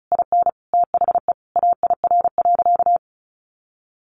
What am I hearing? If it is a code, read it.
Morse code, heard as a keyed tone, about 34 wpm, 700 Hz. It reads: IN THE AIR.
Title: Sentences from Top 900 Words 35wpm